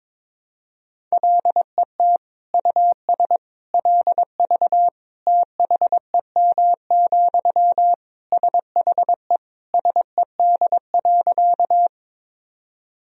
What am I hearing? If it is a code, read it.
Morse code, heard as a keyed tone, about 22 wpm, 700 Hz. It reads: LET US LV THEM, SHE SED.
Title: Sentences from Top 400 Words and Abbreviations 22wpm